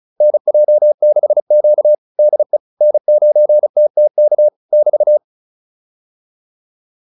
Morse code, heard as a keyed tone, about 34 wpm, 600 Hz. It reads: NJ6Q DE N9TTK <BT>
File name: QSO205 35